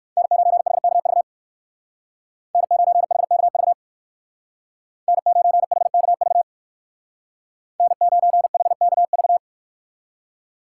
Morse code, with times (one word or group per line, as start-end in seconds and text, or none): 0.2-1.2 N9HKV
2.5-3.7 N9HKV
5.1-6.4 N9HKV
7.8-9.4 N9HKV